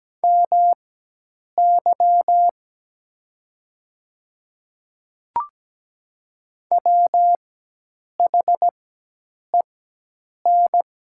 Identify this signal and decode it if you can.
Morse code, heard as a keyed tone, about 17 wpm, spaced as for 8 wpm, 700 Hz. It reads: MY WHEN
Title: 4x Character Spacing - Top 200 Words - 17wpm